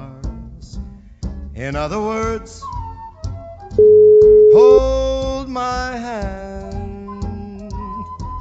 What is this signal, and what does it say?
Ringback tone (Europe, 425 Hz).